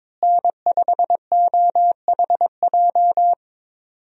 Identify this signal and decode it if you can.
Morse code, heard as a keyed tone, about 22 wpm, 700 Hz. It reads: N5OHJ
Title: USA Call Signs - 5 Characters - Repeated 3x - 22wpm